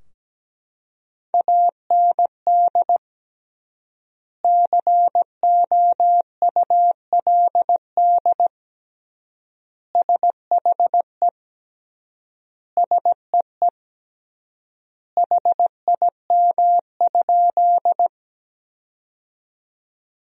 Morse code, heard as a keyed tone, about 17 wpm, 700 Hz. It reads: AND COULD SHE SEE HIM?
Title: 3x Word-Spacing - Sentences from Top 100 Words - 17wpm